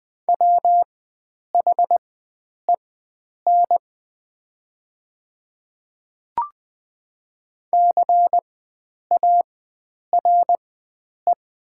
Morse code, heard as a keyed tone, about 20 wpm, spaced as for 9 wpm, 700 Hz. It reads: WHEN CARE
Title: 4x Character Spacing - Top 300 Words - 20wpm